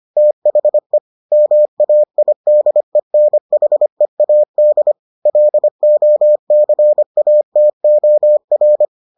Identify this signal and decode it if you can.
Morse code, heard as a keyed tone, about 25 wpm, 600 Hz. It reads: THE MAIDENHEAD LOCATOR